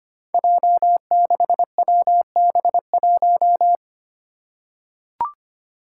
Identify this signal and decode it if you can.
Morse code, heard as a keyed tone, about 25 wpm, 700 Hz. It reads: J6WB1
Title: Five Challenging Letter-Number Combinations 25wpm